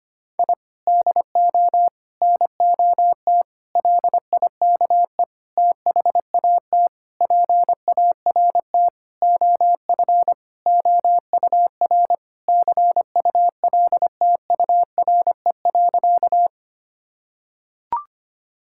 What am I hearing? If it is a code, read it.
Morse code, heard as a keyed tone, about 25 wpm, 700 Hz. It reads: I DO NOT LIKE THAT PART OF OUR CULTURE.
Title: Sentences from Top 1000 Words - Review Part 6 - 25wpm